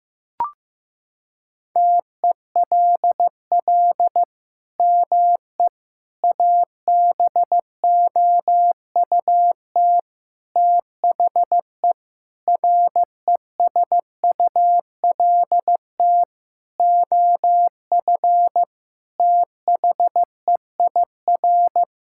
Morse code, heard as a keyed tone, about 15 wpm, 700 Hz. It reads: TELL ME ABOUT THE RESULT OF THEIR
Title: Sentences from Top 1000 Words - Review Part 5 - 15wpm